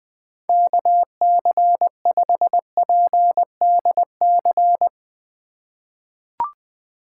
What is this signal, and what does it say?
Morse code, heard as a keyed tone, about 20 wpm, 700 Hz. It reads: KC5PDC
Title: USA Call Signs - 6 Characters - 3x Character Spacing - 20wpm